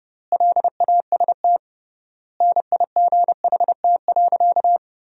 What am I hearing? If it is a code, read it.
Morse code, heard as a keyed tone, about 30 wpm, 700 Hz. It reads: LAST NIGHT.
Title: 3x Word-Spacing - Sentences from Top 300 Words - 30wpm